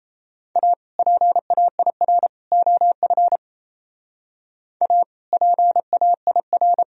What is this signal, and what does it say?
Morse code, heard as a keyed tone, about 34 wpm, 700 Hz. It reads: A PAIR OF A PAIR
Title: Speed-Racing - Sets of 3 Words 28wpm